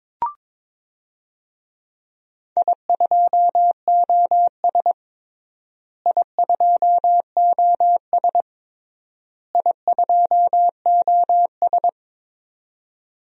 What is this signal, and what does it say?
Morse code, heard as a keyed tone, about 22 wpm, 700 Hz. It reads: I2OS I2OS I2OS
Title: DXCC-100 Call Signs - 4 Characters - Repeated 3x - 22wpm